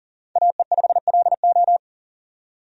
Morse code, heard as a keyed tone, about 40 wpm, 700 Hz. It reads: AE5LO
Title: USA Call Signs - 5 Characters - 40wpm